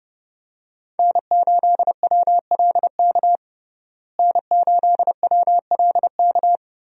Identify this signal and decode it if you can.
Morse code, heard as a keyed tone, about 30 wpm, 700 Hz. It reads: N8WLK N8WLK
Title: USA Call Signs - 5 Characters - Repeated 3x - 30wpm